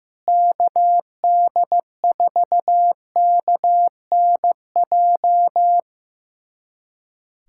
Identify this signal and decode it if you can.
Morse code, heard as a keyed tone, about 15 wpm, 700 Hz. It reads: KD4KNJ
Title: USA Callsign DE Callsign 15wpm